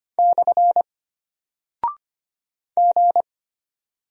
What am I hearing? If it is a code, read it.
Morse code, heard as a keyed tone, about 25 wpm, 700 Hz. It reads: / G
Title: Single Character (Letter:Number:Punctuation:Prosign) Alt 25wpm